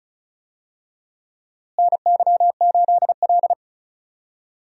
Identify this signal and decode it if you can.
Morse code, heard as a keyed tone, about 36 wpm, 700 Hz. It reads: NY8L
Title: USA Call Signs - 4 Characters - Repeated 2x - 35wpm